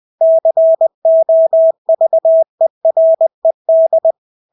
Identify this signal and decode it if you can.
Morse code, heard as a keyed tone, about 20 wpm, 650 Hz. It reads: COVERED
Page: Engelske ord 20 wpm